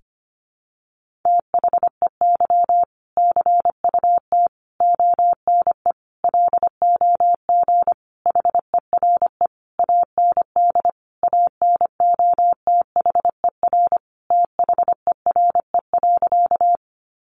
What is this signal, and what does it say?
Morse code, heard as a keyed tone, about 25 wpm, 700 Hz. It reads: THEY CUT ONE LOG HERE AND ANOTHER THERE.
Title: Sentences from Top 1400 Words 25wpm